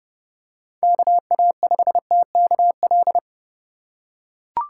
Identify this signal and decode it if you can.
Morse code, heard as a keyed tone, about 30 wpm, 700 Hz. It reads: KA5TKL
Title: USA Call Signs - 6 Characters - 4x Character Spacing - 30wpm